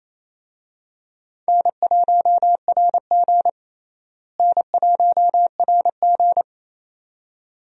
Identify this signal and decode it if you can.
Morse code, heard as a keyed tone, about 28 wpm, 700 Hz. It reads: N1RG N1RG
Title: USA Call Signs - 4 Characters - Repeated 2x - 28wpm